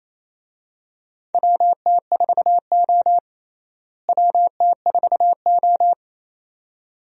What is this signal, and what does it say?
Morse code, heard as a keyed tone, about 28 wpm, 700 Hz. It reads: WT4O WT4O
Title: USA Call Signs - 4 Characters - Repeated 2x - 28wpm